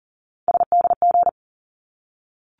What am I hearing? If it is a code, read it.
Morse code, heard as a keyed tone, about 40 wpm, 700 Hz. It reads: SDG
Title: ARRL Sections 40wpm